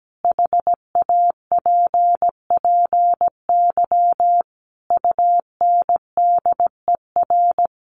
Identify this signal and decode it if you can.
Morse code, heard as a keyed tone, about 17 wpm, 700 Hz. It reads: HAPPY UNDER
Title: Sentences from Top 1500 Words 17wpm